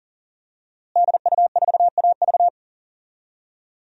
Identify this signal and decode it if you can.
Morse code, heard as a keyed tone, about 40 wpm, 700 Hz. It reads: DU4AV
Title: DXCC-100 Call Signs - 5 Characters - Repeated 2x - 40wpm